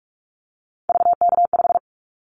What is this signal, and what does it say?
Morse code, heard as a keyed tone, about 46 wpm, 700 Hz. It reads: VK5
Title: Callsign Prefixes for Top 50 DXCC - 45wpm